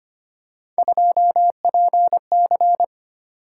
Morse code, heard as a keyed tone, about 25 wpm, 700 Hz. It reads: 2PC
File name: Three Letter-Number Combinations - 25wpm